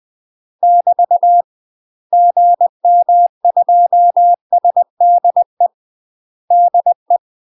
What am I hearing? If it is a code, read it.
Morse code, heard as a keyed tone, about 20 wpm, 700 Hz. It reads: <BT> GM2SDE DE
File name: UK MASTER QSOs TRAINING / 20WPM / UK MASTER QSOs 0029